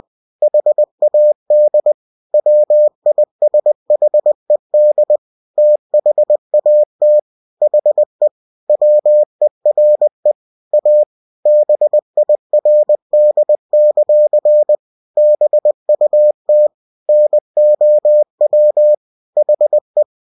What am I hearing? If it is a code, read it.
Morse code, heard as a keyed tone, about 20 wpm, 600 Hz. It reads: HAD WISHED THAT HE WERE A BIRD; BUT NOW HE
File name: cw-books / winnie-the-pooh / winnie-20wpm-page-0070